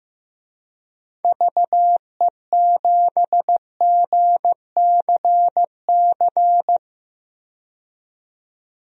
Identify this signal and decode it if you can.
Morse code, heard as a keyed tone, about 15 wpm, 700 Hz. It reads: VE7GCC
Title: DXCC-100 Call Signs - 6 Characters - 15wpm